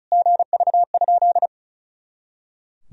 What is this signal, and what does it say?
Morse code, heard as a keyed tone, about 34 wpm, 700 Hz. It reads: GV?
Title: Sentences from Top 200 Words and Abbreviations 35wpm